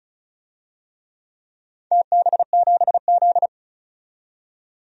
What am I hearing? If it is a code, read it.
Morse code, heard as a keyed tone, about 34 wpm, 700 Hz. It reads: TB7Z
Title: DXCC-100 Call Signs - 4 Characters - 35wpm